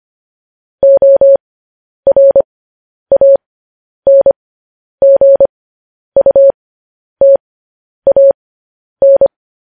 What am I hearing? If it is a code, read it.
Morse code, heard as a keyed tone, about 25 wpm, spaced as for 10 wpm, 550 Hz. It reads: ORANGUTAN
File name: podcast / News headlines-1x-25x10